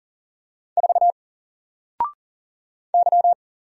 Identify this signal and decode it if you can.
Morse code, heard as a keyed tone, about 40 wpm, 700 Hz. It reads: 4 Y
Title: Single Character (Letter:Number:Punctuation:Prosign) 40wpm